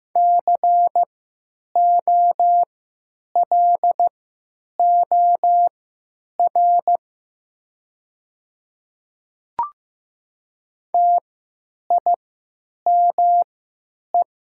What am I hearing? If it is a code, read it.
Morse code, heard as a keyed tone, about 15 wpm, spaced as for 8 wpm, 700 Hz. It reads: COLOR TIME
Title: 3x Character Spacing - Top 500 Words - 15wpm